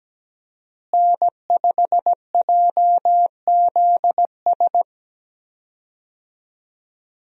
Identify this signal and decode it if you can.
Morse code, heard as a keyed tone, about 17 wpm, 700 Hz. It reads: N5JZS
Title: USA Call Signs - 5 Characters - Repeated 2x - 17wpm